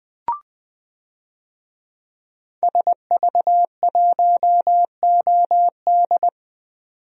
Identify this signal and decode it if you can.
Morse code, heard as a keyed tone, about 20 wpm, 700 Hz. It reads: SV1OD
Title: DXCC-100 Call Signs - 5 Characters - 20wpm